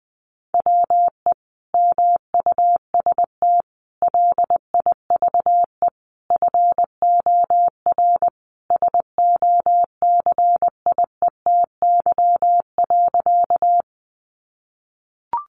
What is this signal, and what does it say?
Morse code, heard as a keyed tone, about 20 wpm, 700 Hz. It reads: WE MUST LIVE FOR SOCIETY.
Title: Sentences from Top 700 Words 20wpm